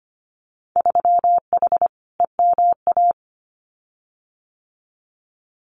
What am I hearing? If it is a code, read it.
Morse code, heard as a keyed tone, about 25 wpm, 700 Hz. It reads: 3H EMA
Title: Winter Field Day Exchanges Encore 25wpm